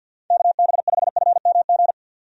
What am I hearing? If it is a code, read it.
Morse code, heard as a keyed tone, about 51 wpm, 700 Hz. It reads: KB5FMG
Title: USA Call Signs - 6 Characters - 50wpm